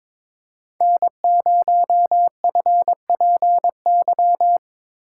Morse code, heard as a keyed tone, about 22 wpm, 700 Hz. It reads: N0FPY
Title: USA Call Signs - 5 Characters - 2x Character Spacing - 22wpm